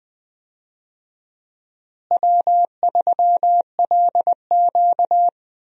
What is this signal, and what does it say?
Morse code, heard as a keyed tone, about 20 wpm, 700 Hz. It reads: W3LQ
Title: USA Call Signs - 4 Characters - Repeated 2x - 20wpm